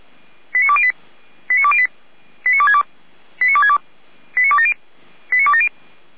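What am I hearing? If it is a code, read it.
Five-tone selective call (ZVEI), heard as quick runs of five tones, tones of about 70 ms. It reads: 89208 89208 89262 89262 89270 89270
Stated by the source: ZVEI five-tone selective call 89340 received by SDR.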